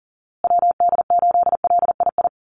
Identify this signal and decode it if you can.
Morse code, heard as a keyed tone, about 40 wpm, 700 Hz. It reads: WD8LII
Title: USA Call Signs Repeated 2x 40wpm